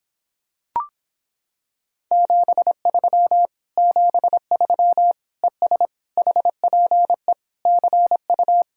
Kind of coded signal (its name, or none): Morse code